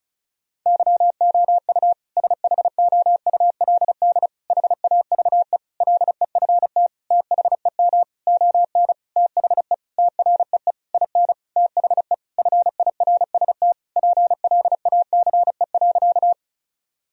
Morse code, heard as a keyed tone, about 36 wpm, 700 Hz. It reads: YOU SHOULD HAVE LEFT THEM ON THE TREE IN THE FIRST PLACE.
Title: Sentences from Top 1000 Words - Review Part 1 - 35wpm